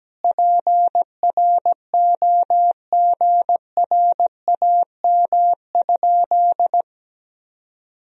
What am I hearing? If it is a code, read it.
Morse code, heard as a keyed tone, about 17 wpm, 700 Hz. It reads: PROGRAM?
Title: Sentences from Top 1000 Words - Review Part 4 - 17wpm